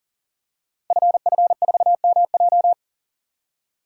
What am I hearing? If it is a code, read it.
Morse code, heard as a keyed tone, about 41 wpm, 690 Hz. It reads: FF4MJ
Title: Code Groups - 5 Letter:Numbers - 40wpm